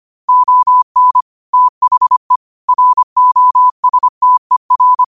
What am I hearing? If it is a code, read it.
Morse code, heard as a keyed tone, about 25 wpm, 1000 Hz. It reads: ON THE ROSTER
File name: 25wpm